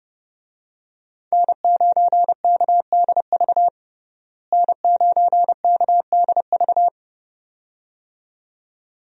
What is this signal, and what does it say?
Morse code, heard as a keyed tone, about 30 wpm, 700 Hz. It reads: N9KDV N9KDV
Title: USA Call Signs - 5 Characters - Repeated 2x - 30wpm